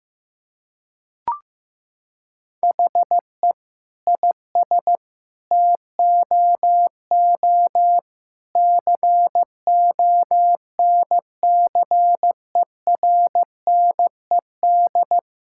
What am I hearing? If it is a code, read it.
Morse code, heard as a keyed tone, about 15 wpm, 700 Hz. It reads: HE IS TOO CONCERNED
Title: Sentences from Top 1000 Words - Review Part 6 - 15wpm